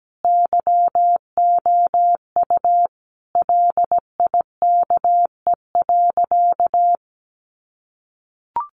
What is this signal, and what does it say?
Morse code, heard as a keyed tone, about 17 wpm, 700 Hz. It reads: YOU LIKE.
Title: Sentences from Top 500 Words 17wpm